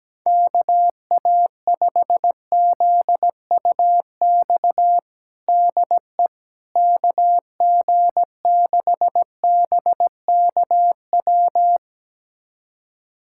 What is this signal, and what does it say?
Morse code, heard as a keyed tone, about 17 wpm, 700 Hz. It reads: KA5ZUX DE KG6BKW
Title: USA Callsign DE Callsign 17wpm